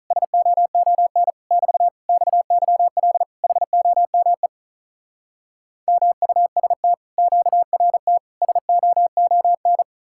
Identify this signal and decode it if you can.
Morse code, heard as a keyed tone, about 47 wpm, 700 Hz. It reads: SOON <BT> XYL HOME MUST QRT SOON